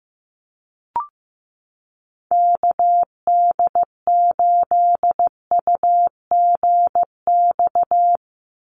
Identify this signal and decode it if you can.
Morse code, heard as a keyed tone, about 15 wpm, 700 Hz. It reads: KD8UGX
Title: USA Call Signs 15wpm